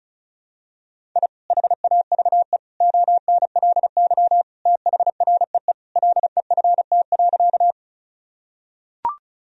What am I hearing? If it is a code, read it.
Morse code, heard as a keyed tone, about 36 wpm, 700 Hz. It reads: I HAVE ONLY THREE LEFT.
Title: Sentences from Top 1000 Words - Review Part 1 - 35wpm